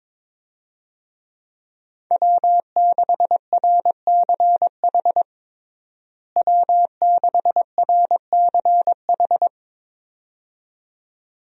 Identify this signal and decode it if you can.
Morse code, heard as a keyed tone, about 22 wpm, 700 Hz. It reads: W6RCH W6RCH
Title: USA Call Signs - 5 Characters - Repeated 2x - 22wpm